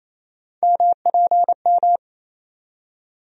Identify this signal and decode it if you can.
Morse code, heard as a keyed tone, about 28 wpm, 700 Hz. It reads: MPM
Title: USA Call Sign Postfixes - 3 Letters 28wpm